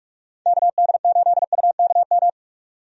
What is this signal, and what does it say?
Morse code, heard as a keyed tone, about 41 wpm, 700 Hz. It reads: KD8UKM
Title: USA Call Signs - 6 Characters - Repeated 2x - 45wpm